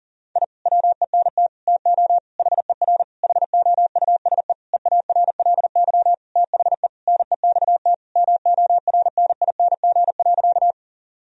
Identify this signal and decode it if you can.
Morse code, heard as a keyed tone, about 40 wpm, 700 Hz. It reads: I WENT TO HER HOUSE EARLY THE NEXT MORNING.
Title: Sentences from Top 1000 Words - Review Part 2 - 40wpm